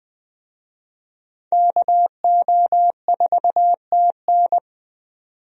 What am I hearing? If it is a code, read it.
Morse code, heard as a keyed tone, about 20 wpm, 700 Hz. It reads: KO4TN
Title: USA Call Signs - 5 Characters - 20wpm